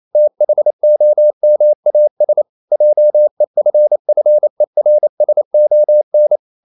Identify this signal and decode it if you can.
Morse code, heard as a keyed tone, about 28 wpm, 600 Hz. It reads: THOMAS JEFFERSON
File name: CW / Practice / EnglishQuotations0128